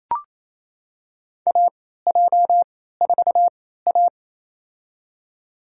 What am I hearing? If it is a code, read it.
Morse code, heard as a keyed tone, about 28 wpm, spaced as for 16 wpm, 700 Hz. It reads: AJ4A
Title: USA Call Signs - 4 Characters - 3x Character Spacing - 28wpm